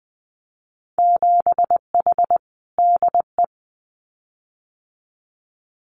Morse code, heard as a keyed tone, about 20 wpm, 700 Hz. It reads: 7H DE
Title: Winter Field Day Exchanges Encore 20wpm